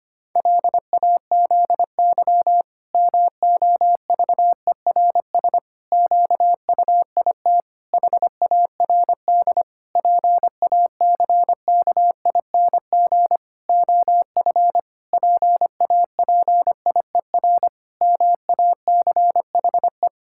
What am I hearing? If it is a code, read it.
Morse code, heard as a keyed tone, about 25 wpm, 700 Hz. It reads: LAZY MOVERS QUIT HARD PACKING OF PAPIER MACHE
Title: Pangram Challenge v2 - 25wpm